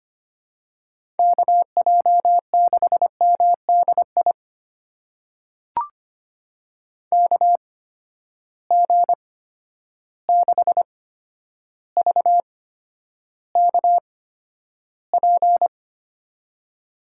Morse code, heard as a keyed tone, about 25 wpm, spaced as for 11 wpm, 700 Hz. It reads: KJ6MDI KG6VKP
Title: USA Call Signs - 6 Characters - 8x Character Spacing - 25wpm